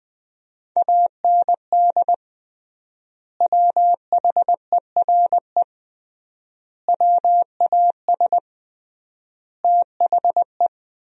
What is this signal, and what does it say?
Morse code, heard as a keyed tone, about 21 wpm, 700 Hz. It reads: AND WHERE WAS THE
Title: 3x Word-Spacing - Sentences from Top 200 Words - 20wpm